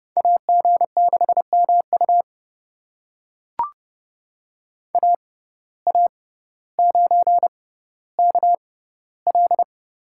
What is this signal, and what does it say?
Morse code, heard as a keyed tone, about 30 wpm, spaced as for 15 wpm, 700 Hz. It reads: AG6MU AA9KL
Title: USA Call Signs - 5 Characters - 6x Character Spacing - 30wpm